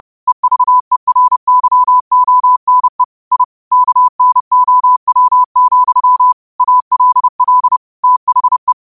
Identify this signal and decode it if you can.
Morse code, heard as a keyed tone, about 30 wpm, 1000 Hz. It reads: EVERYONE I KNOW, ALL THE
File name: qotdfiles / 30wpm